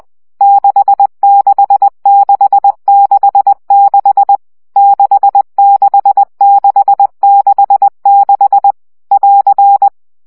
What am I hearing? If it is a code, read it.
Morse code, heard as a keyed tone, about 20 wpm, 800 Hz. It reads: 66666 66666 <AR>
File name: CW gyakorlas 6 17 baud